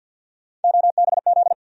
Morse code, heard as a keyed tone, about 52 wpm, 700 Hz. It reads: OBZ
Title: USA Call Sign Postfixes - 3 Letters 50wpm